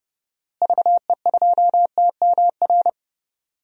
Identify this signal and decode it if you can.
Morse code, heard as a keyed tone, about 30 wpm, 700 Hz. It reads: VE2TMR
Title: DXCC-100 Call Signs - 6 Characters - Repeated 3x - 30wpm